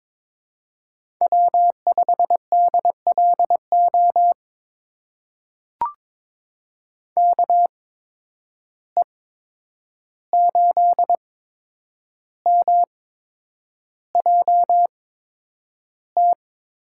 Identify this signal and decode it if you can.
Morse code, heard as a keyed tone, about 22 wpm, spaced as for 9 wpm, 700 Hz. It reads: W5DLO KE8MJT
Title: USA Call Signs - 8x Character Spacing - 22wpm